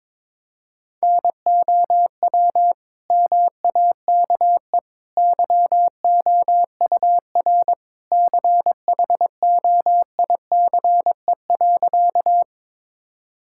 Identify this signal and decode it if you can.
Morse code, heard as a keyed tone, about 22 wpm, 700 Hz. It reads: NOW MAKE YOUR CHOICE.
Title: Sentences from Top 1000 Words - Review Part 6 - 22wpm